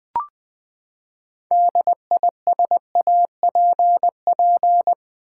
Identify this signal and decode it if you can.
Morse code, heard as a keyed tone, about 20 wpm, 700 Hz. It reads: DISAPP
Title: Build A Word - 6 Letters At A Time - 12-Letter Words - 20wpm